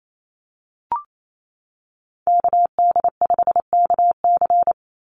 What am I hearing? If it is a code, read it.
Morse code, heard as a keyed tone, about 28 wpm, 700 Hz. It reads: KD5KC
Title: NA SOTA Activator Callsigns 28wpm